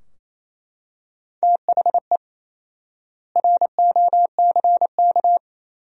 Morse code, heard as a keyed tone, about 28 wpm, 700 Hz. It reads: THE ROCK
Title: 4x Word-Spacing - Sentences from Top 400 Words - 28wpm